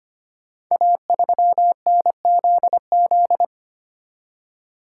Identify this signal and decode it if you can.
Morse code, heard as a keyed tone, about 25 wpm, 700 Hz. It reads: A3NZZ
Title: Code Groups - 5 Letter:Numbers - 25wpm